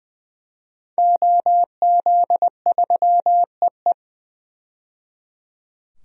Morse code, heard as a keyed tone, about 20 wpm, 700 Hz. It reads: OZ3EE
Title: DXCC-100 Call Signs - 5 Characters - 20wpm